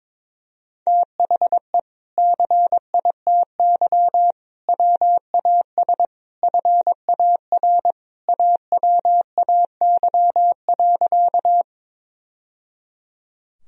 Morse code, heard as a keyed tone, about 22 wpm, 700 Hz. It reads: THE CITY WAS FAR AWAY.
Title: Sentences from Top 1000 Words - Review Part 1 - 22wpm